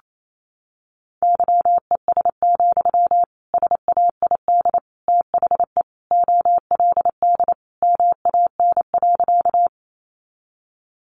Morse code, heard as a keyed tone, about 28 wpm, 700 Hz. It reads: YES, SAID THE OLD MAN.
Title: Sentences from Top 500 Words 28wpm